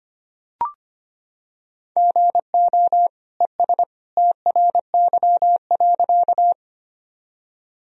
Morse code, heard as a keyed tone, about 25 wpm, 700 Hz. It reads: GO ES TRY.
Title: Sentences from Top 200 Words and Abbreviations 25wpm